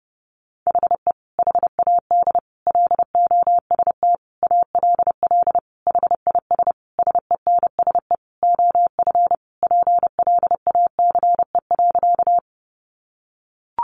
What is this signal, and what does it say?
Morse code, heard as a keyed tone, about 30 wpm, 700 Hz. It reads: HE HAD LOST ALL HIS SENSE OF PLACE.
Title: Sentences from Top 1400 Words 30wpm